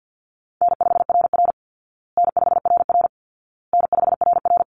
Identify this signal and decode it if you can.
Morse code, heard as a keyed tone, about 48 wpm, 700 Hz. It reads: N5RR N5RR N5RR
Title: Call Signs Repeated 2x - 50wpm